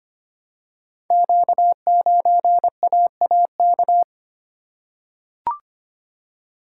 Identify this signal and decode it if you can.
Morse code, heard as a keyed tone, about 25 wpm, 700 Hz. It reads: Q9AAK
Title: Code Groups - 5 Letter:Numbers - 25wpm